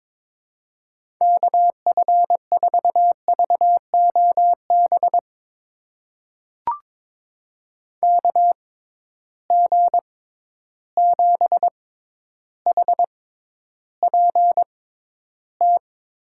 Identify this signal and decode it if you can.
Morse code, heard as a keyed tone, about 22 wpm, spaced as for 11 wpm, 700 Hz. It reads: KF4VOB KG7HPT
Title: USA Call Signs - 6 Characters - 6x Character Spacing - 22wpm